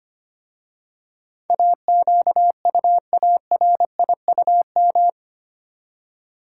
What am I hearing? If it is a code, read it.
Morse code, heard as a keyed tone, about 25 wpm, 700 Hz. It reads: AQUARIUM
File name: practice / 25wpm / Easy Words with Hints - 25wpm